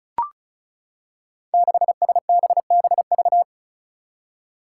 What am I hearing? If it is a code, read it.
Morse code, heard as a keyed tone, about 36 wpm, 700 Hz. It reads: 6SBBV
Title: Five Challenging Letter-Number Combinations 35wpm